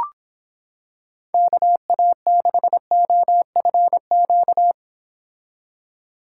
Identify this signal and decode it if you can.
Morse code, heard as a keyed tone, about 26 wpm, 700 Hz. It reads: KA6OFQ